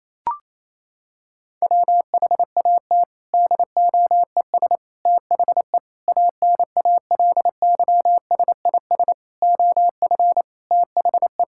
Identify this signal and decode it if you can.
Morse code, heard as a keyed tone, about 28 wpm, 700 Hz. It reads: WHAT DOES THE ANALYSIS OF THE
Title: Sentences from Top 1000 Words - Review Part 6 - 28wpm